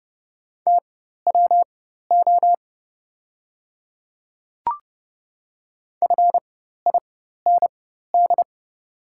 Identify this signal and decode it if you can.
Morse code, heard as a keyed tone, about 30 wpm, spaced as for 14 wpm, 700 Hz. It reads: TWO FIND
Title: 4x Character Spacing - Top 200 Words - 30wpm